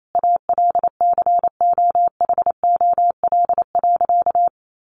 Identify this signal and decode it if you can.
Morse code, heard as a keyed tone, about 28 wpm, 700 Hz. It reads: ALCOHOL.
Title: Sentences from Top 1400 Words 28wpm